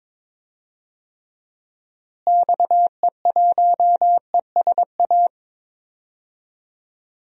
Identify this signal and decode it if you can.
Morse code, heard as a keyed tone, about 22 wpm, 700 Hz. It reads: XE1ESA
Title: DXCC-100 Call Signs - 6 Characters - Repeated 2x - 22wpm